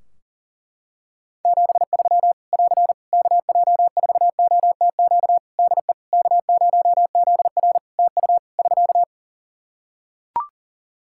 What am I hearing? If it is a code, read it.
Morse code, heard as a keyed tone, about 40 wpm, 700 Hz. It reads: 73 <AR> KJ4OTQ DE K0ZR TU <SK>
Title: Practice QSO - Exchange 4 - Ending - 40wpm